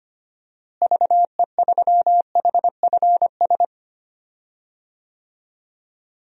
Morse code, heard as a keyed tone, about 25 wpm, 700 Hz. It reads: VE3HFS
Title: DXCC-100 Call Signs - 6 Characters - Repeated 3x - 25wpm